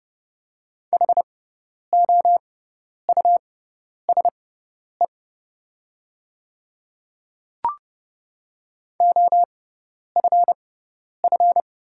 Morse code, heard as a keyed tone, about 29 wpm, spaced as for 10 wpm, 700 Hz. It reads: HOUSE OFF
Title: 6x Character Spacing - Top 200 Words - 30wpm